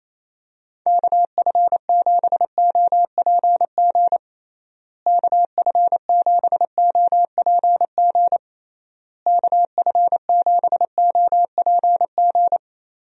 Morse code, heard as a keyed tone, about 28 wpm, 700 Hz. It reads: KF7OPG KF7OPG KF7OPG
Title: USA Call Signs - 6 Characters - Repeated 3x - 28wpm